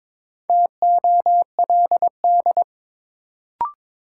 Morse code, heard as a keyed tone, about 22 wpm, 700 Hz. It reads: TOLD
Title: Sentences from Top 400 Words - New Word Drill - 22wpm